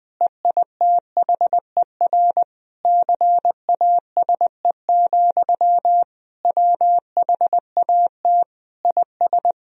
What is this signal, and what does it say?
Morse code, heard as a keyed tone, about 20 wpm, 700 Hz. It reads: EITHER CASE, WHAT IS
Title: Sentences from Top 1000 Words - Review Part 4 - 20wpm